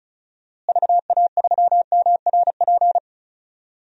Morse code, heard as a keyed tone, about 34 wpm, 700 Hz. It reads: VA3MRP
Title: DXCC-100 Call Signs - 6 Characters - Repeated 2x - 35wpm